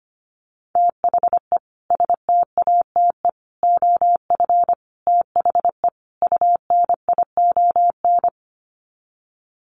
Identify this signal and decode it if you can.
Morse code, heard as a keyed tone, about 25 wpm, 690 Hz. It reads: THE STATE OF THE UNION
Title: Sets of 5 Words Encore 25wpm